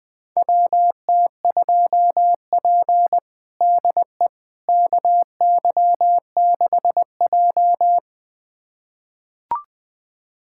Morse code, heard as a keyed tone, about 20 wpm, 700 Hz. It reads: WT2P DE KY6J
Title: USA Callsign DE Callsign - 4 Characters - 20wpm